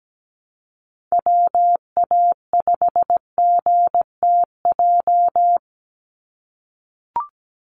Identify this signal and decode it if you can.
Morse code, heard as a keyed tone, about 17 wpm, 700 Hz. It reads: WA5GTJ
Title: USA Call Signs 17wpm